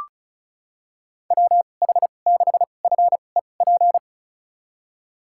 Morse code, heard as a keyed tone, about 36 wpm, spaced as for 25 wpm, 700 Hz. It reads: WH6FEP
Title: USA Call Signs - 6 Characters - 2x Character Spacing - 35wpm